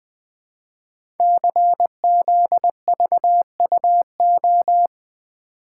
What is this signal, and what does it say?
Morse code, heard as a keyed tone, about 20 wpm, 700 Hz. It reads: CZVUO
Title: Code Groups - 5 Letter:Numbers - 20wpm